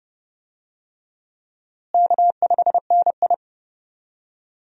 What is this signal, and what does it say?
Morse code, heard as a keyed tone, about 30 wpm, 700 Hz. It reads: K5NI
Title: USA Call Signs - 4 Characters - Repeated 3x - 30wpm